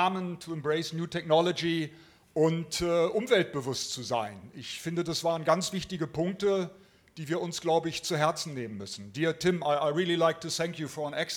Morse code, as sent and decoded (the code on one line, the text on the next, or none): none